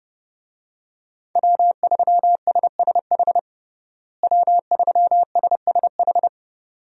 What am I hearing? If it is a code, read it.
Morse code, heard as a keyed tone, about 30 wpm, 700 Hz. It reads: W3SSH W3SSH
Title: USA Call Signs - 5 Characters - Repeated 2x - 30wpm